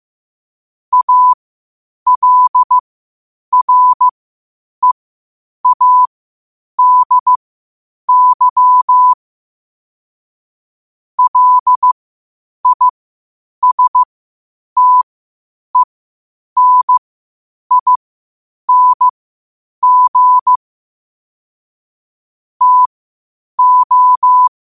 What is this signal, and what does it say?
Morse code, heard as a keyed tone, about 15 wpm, spaced as for 8 wpm, 1000 Hz. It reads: ALREADY LISTENING TO
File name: qotdfiles / 8wpm